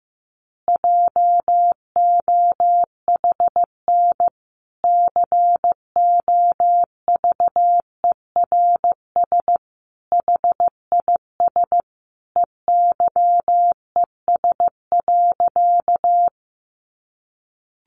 Morse code, heard as a keyed tone, about 15 wpm, 700 Hz. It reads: JOHN COVERS HIS EYES.
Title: Easy Phrases 15wpm